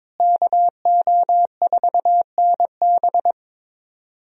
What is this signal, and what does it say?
Morse code, heard as a keyed tone, about 22 wpm, 700 Hz. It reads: KO4NB
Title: USA Call Signs - 5 Characters - Repeated 3x - 22wpm